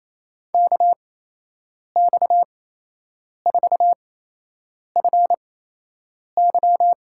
Morse code, heard as a keyed tone, about 28 wpm, 700 Hz. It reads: K X 4 F Y
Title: USA Call Signs - 5 Characters - 8x Character Spacing - 28wpm